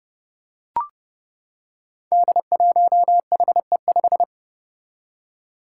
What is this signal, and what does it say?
Morse code, heard as a keyed tone, about 28 wpm, 700 Hz. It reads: D1HE5
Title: Five Challenging Letter-Number Combinations 30wpm